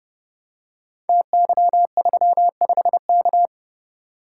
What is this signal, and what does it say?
Morse code, heard as a keyed tone, about 30 wpm, 700 Hz. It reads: TY35K
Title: Code Groups - 5 Letter:Numbers - 30wpm